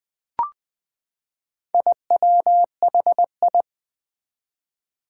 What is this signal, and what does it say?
Morse code, heard as a keyed tone, about 20 wpm, 700 Hz. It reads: IWHI
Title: Four Challenging Letter-Number Combinations 20wpm